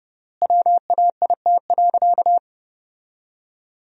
Morse code, heard as a keyed tone, about 30 wpm, 700 Hz. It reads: WAIT.
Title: 3x Word-Spacing - Sentences from Top 500 Words - 30wpm